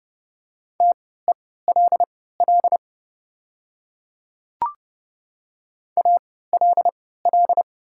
Morse code, heard as a keyed tone, about 31 wpm, spaced as for 17 wpm, 700 Hz. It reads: TELL ALL
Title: 3x Character Spacing - Top 500 Words - 30wpm